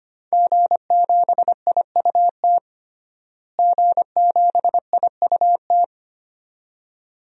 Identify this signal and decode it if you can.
Morse code, heard as a keyed tone, about 25 wpm, 700 Hz. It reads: G7IUT G7IUT
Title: DXCC-100 Call Signs - 5 Characters - Repeated 2x - 25wpm